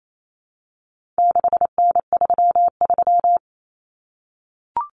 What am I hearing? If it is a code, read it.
Morse code, heard as a keyed tone, about 28 wpm, 700 Hz. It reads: 6N33
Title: Common Cut Numbers - 4-Digit 28wpm